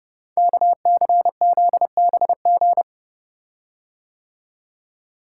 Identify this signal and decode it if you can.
Morse code, heard as a keyed tone, about 30 wpm, 700 Hz. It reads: KCZBG
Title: Code Groups - 5 Letters - 30wpm